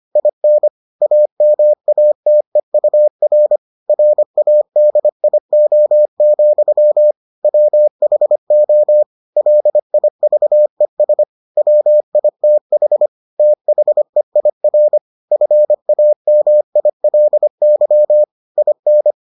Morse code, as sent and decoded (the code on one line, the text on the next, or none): .. -. / .- -- .- - . ..- .-. / .-. .- -.. .. --- --..-- / .-- .... --- / .-.. .. ...- . ... / .-- .. - .... / - .... .. .-. / ..-. .- -- .. .-.. -.-- / .. -.
IN AMATEUR RADIO, WHO LIVES WITH THEIR FAMILY IN